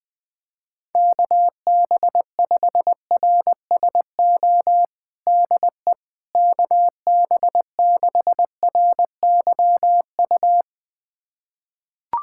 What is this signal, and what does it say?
Morse code, heard as a keyed tone, about 20 wpm, 700 Hz. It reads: KB5RSO DE KB6RYU
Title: USA Callsign DE Callsign - 6 Characters - 20wpm